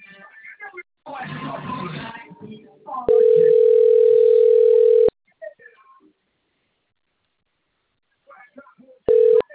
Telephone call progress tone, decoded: ringback tone

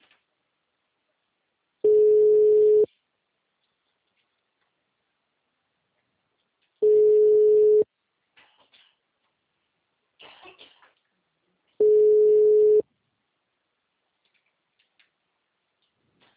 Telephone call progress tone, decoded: ringback tone

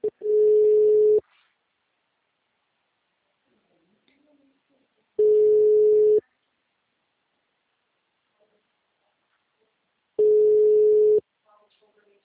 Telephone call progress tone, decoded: ringback tone

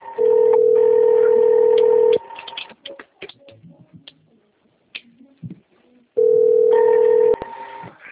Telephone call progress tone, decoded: ringback tone

